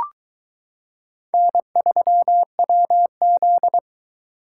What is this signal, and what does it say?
Morse code, heard as a keyed tone, about 22 wpm, 700 Hz. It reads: N3WZ